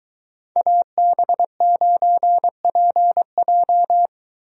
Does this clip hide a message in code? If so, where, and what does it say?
Yes: Morse code from 0.6 to 4.1 s, heard as a keyed tone, about 22 wpm, 700 Hz. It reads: AB9PJ